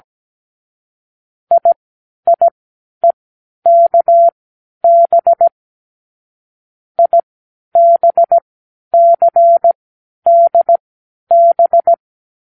Morse code, heard as a keyed tone, about 17 wpm, spaced as for 11 wpm, 680 Hz. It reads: IIEKB IBCDB
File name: k7qo / Lesson18(A-L)